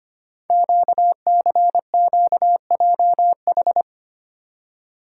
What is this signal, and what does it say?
Morse code, heard as a keyed tone, about 25 wpm, 700 Hz. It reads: QCQJH
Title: Code Groups - 5 Letters - 25wpm